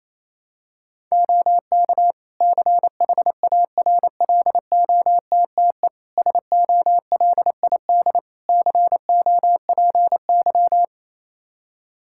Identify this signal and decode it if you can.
Morse code, heard as a keyed tone, about 28 wpm, 700 Hz. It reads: OK CHARLOTTE SOLID COPY
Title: Practice QSO - Exchange 3 - More Info - 28wpm